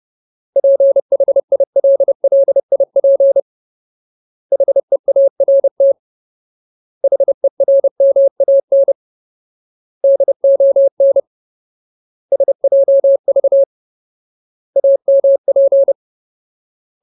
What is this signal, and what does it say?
Morse code, heard as a keyed tone, about 30 wpm, 550 Hz. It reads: PHILLIP HEART HERMAN DON SJV AMP
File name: Ultimate Word List / 30WPM Ultimate